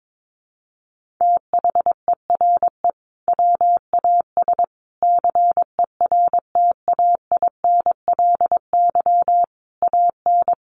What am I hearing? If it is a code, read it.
Morse code, heard as a keyed tone, about 22 wpm, 700 Hz. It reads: THERE WAS CERTAINLY AN
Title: Sentences from Top 1400 Words 22wpm